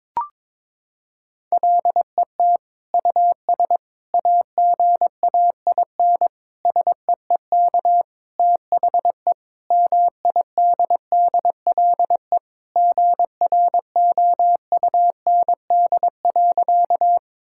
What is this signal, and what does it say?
Morse code, heard as a keyed tone, about 22 wpm, 700 Hz. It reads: LET US AGAIN SEEK THE MIDDLE GROUND.
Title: Sentences from Top 1600 Words 22wpm